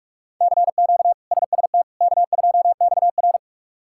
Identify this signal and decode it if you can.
Morse code, heard as a keyed tone, about 46 wpm, 700 Hz. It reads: CQ SST K2XR
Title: K1USN SST QSO Practice - 45wpm